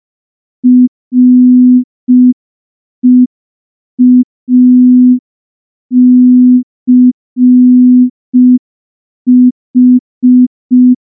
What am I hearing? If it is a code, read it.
Morse code, heard as a keyed tone, about 5 wpm, 250 Hz. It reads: REACH
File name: morse / morse-training-05WPM